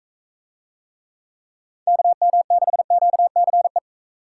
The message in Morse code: KM6QCE

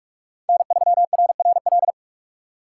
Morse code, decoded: N3RRL

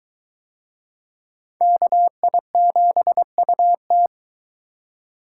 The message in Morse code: KI7UT

